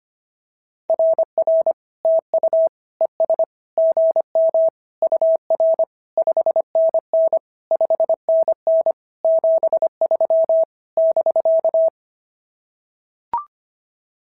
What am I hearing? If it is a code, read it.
Morse code, heard as a keyed tone, about 25 wpm, 690 Hz. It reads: RR TU ES GM UR 5NN 5NN 73 <BK>
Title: SOTA QSO Practice 25wpm